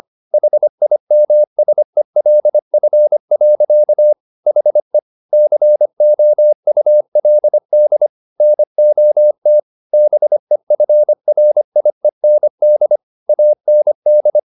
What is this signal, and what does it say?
Morse code, heard as a keyed tone, about 25 wpm, 600 Hz. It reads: HIMSELF. HE COULD NOT BEFRIEND AND